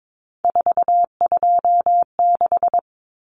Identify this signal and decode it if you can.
Morse code, heard as a keyed tone, about 22 wpm, 700 Hz. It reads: 426
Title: Common Cut Numbers - 3-Digit 22wpm